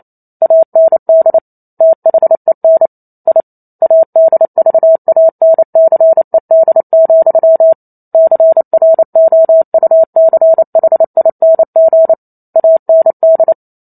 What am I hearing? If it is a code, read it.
Morse code, heard as a keyed tone, about 28 wpm, 650 Hz. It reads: AND THEN I ADVANCED, CROUCHING AND